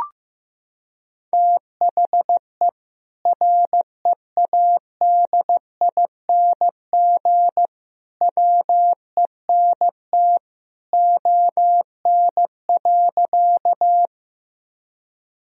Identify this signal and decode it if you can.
Morse code, heard as a keyed tone, about 15 wpm, 700 Hz. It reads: THE READING WENT ON.
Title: Sentences from Top 1000 Words - Review Part 3 - 15wpm